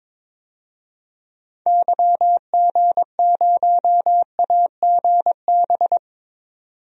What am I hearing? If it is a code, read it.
Morse code, heard as a keyed tone, about 22 wpm, 700 Hz. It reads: YG0AGB
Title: DXCC-100 Call Signs - 6 Characters - 22wpm